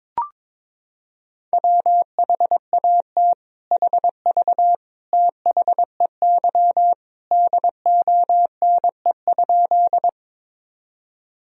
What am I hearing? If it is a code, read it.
Morse code, heard as a keyed tone, about 22 wpm, 700 Hz. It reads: WHAT HV THEY DONE?
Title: Sentences from Top 500 Words and Abbreviations 22wpm